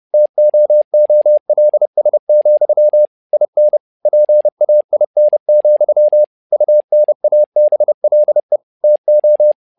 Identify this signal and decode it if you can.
Morse code, heard as a keyed tone, about 30 wpm, 600 Hz. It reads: TOOLS, IN PAIN, UNABLE TO
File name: learning center / tower trouble / tower trouble 30wpm